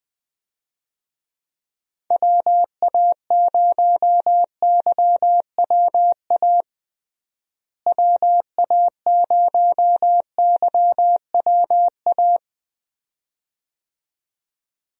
Morse code, heard as a keyed tone, about 20 wpm, 700 Hz. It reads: WA0YWA WA0YWA
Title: USA Call Signs - 6 Characters - Repeated 2x - 20wpm